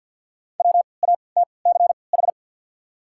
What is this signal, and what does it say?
Morse code, heard as a keyed tone, about 53 wpm, spaced as for 29 wpm, 690 Hz. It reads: WATCH
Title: 3x Character Spacing - Top 300 Words - 50wpm